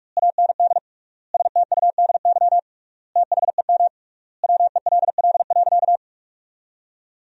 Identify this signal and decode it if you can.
Morse code, heard as a keyed tone, about 46 wpm, 700 Hz. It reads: AND STUDY THEM WELL.
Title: 3x Word-Spacing - Sentences from Top 300 Words - 45wpm